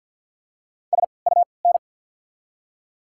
Morse code, heard as a keyed tone, about 53 wpm, spaced as for 29 wpm, 690 Hz. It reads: SUN
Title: 3x Character Spacing - Top 300 Words - 50wpm